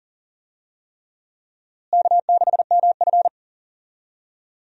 Morse code, heard as a keyed tone, about 40 wpm, 700 Hz. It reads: K6MF
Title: USA Call Signs - 4 Characters - Repeated 3x - 40wpm